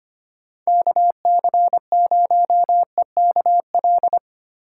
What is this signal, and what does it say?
Morse code, heard as a keyed tone, about 25 wpm, 700 Hz. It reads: KC0EKL
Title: USA Call Signs - 6 Characters - 3x Character Spacing - 25wpm